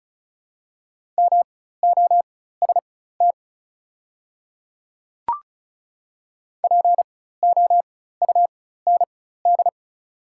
Morse code, heard as a keyed tone, about 36 wpm, spaced as for 16 wpm, 700 Hz. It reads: MOST POUND
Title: 4x Character Spacing - Top 500 Words - 35wpm